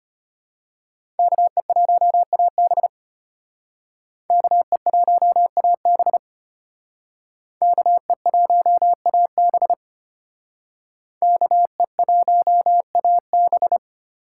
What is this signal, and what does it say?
Morse code, heard as a keyed tone, about 32 wpm, 700 Hz. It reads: KE1AB KE1AB KE1AB KE1AB